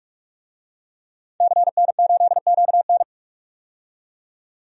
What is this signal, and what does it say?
Morse code, heard as a keyed tone, about 44 wpm, 690 Hz. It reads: CN8QN
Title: DXCC-100 Call Signs - 5 Characters - 45wpm